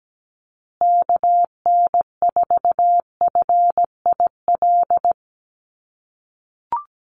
Morse code, heard as a keyed tone, about 17 wpm, 700 Hz. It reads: KN4FIL
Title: USA Call Signs Repeated 3x 17wpm